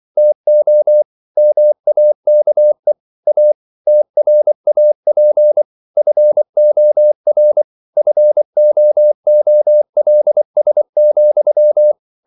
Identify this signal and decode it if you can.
Morse code, heard as a keyed tone, about 24 wpm, 600 Hz. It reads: TO MAKE A TRAP FOR FOOLS,
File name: CW / Practice / If24